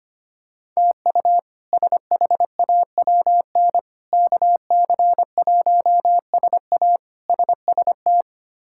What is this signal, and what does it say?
Morse code, heard as a keyed tone, about 25 wpm, 700 Hz. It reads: TU SHAWN KC1SA SST
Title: K1USN SST QSO Practice - 25wpm